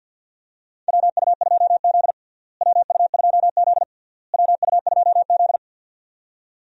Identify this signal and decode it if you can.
Morse code, heard as a keyed tone, about 53 wpm, 700 Hz. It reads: WU2Z WU2Z WU2Z
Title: USA Call Signs - 4 Characters - Repeated 3x - 50wpm